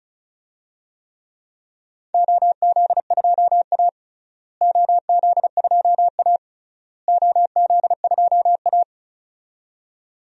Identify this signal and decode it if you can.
Morse code, heard as a keyed tone, about 34 wpm, 700 Hz. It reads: OZ2A OZ2A OZ2A
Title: DXCC-100 Call Signs - 4 Characters - Repeated 3x - 35wpm